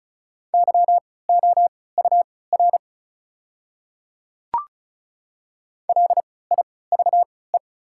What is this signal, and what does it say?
Morse code, heard as a keyed tone, about 35 wpm, spaced as for 20 wpm, 690 Hz. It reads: YOUR LIVE
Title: 3x Character Spacing - Top 300 Words - 35wpm